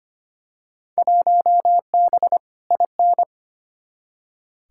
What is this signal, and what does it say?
Morse code, heard as a keyed tone, about 25 wpm, 700 Hz. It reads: 1B IN
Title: Field Day Exchanges Encore - 25wpm